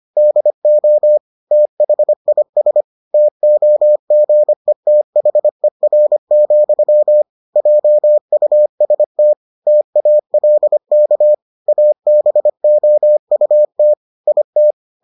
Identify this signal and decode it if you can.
Morse code, heard as a keyed tone, about 25 wpm, 600 Hz. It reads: DO THIS TOGETHER, JUST TALK ABOUT IT